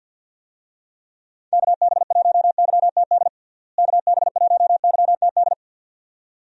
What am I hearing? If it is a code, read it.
Morse code, heard as a keyed tone, about 47 wpm, 700 Hz. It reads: KB1YTD KB1YTD
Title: USA Call Signs - 6 Characters - Repeated 2x - 50wpm